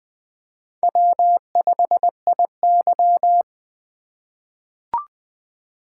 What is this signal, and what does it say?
Morse code, heard as a keyed tone, about 20 wpm, 700 Hz. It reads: W5IY
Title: USA Call Signs - 4 Characters - 2x Character Spacing - 20wpm